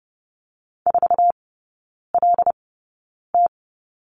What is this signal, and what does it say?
Morse code, heard as a keyed tone, about 30 wpm, spaced as for 9 wpm, 700 Hz. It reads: 4LT
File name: 081 Lesson 14e - Number 2 - Three Characters 30wpm